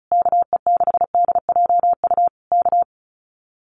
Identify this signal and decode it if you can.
Morse code, heard as a keyed tone, about 36 wpm, 700 Hz. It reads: KE6DJU K
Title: Common QSO Phrases Encore 35wpm